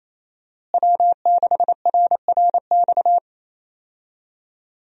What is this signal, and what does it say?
Morse code, heard as a keyed tone, about 27 wpm, 700 Hz. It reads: W6RRX
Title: USA Call Signs - 5 Characters - Repeated 2x - 28wpm